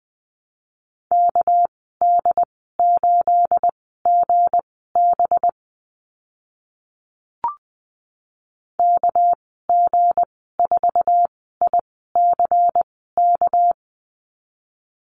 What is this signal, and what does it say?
Morse code, heard as a keyed tone, about 20 wpm, spaced as for 14 wpm, 700 Hz. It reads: KD8GB KG4ICK
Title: 213 Lesson 34g - Number 8 - Call Signs 20wpm